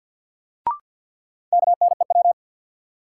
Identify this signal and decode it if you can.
Morse code, heard as a keyed tone, about 53 wpm, 700 Hz. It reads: KNEW
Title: Sentences from Top 400 Words - New Word Drill - 50wpm